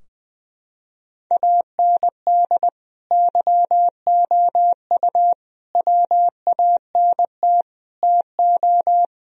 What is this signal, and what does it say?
Morse code, heard as a keyed tone, about 20 wpm, 700 Hz. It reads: AND YOU WANT TO
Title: Sentences from Top 1000 Words - Review Part 4 - 20wpm